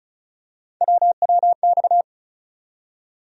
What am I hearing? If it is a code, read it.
Morse code, heard as a keyed tone, about 34 wpm, 700 Hz. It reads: WWX
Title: USA Call Sign Postfixes - 3 Letters 35wpm